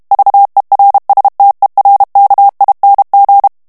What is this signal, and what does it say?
Morse code, heard as a keyed tone, about 32 wpm, 800 Hz. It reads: VERSTERKING
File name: morse mp3 / HOLLANDAIS texte clair 1 vitesse 30 mots minute